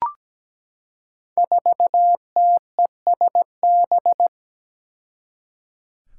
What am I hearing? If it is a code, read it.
Morse code, heard as a keyed tone, about 17 wpm, 700 Hz. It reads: 4TESB
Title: Five Challenging Letter-Number Combinations 17wpm